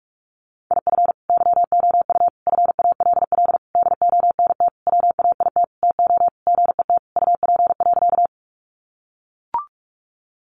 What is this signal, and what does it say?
Morse code, heard as a keyed tone, about 44 wpm, 700 Hz. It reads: IF YOU FALL DONT WAIT TO GET UP.
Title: Sentences from Top 500 Words 45wpm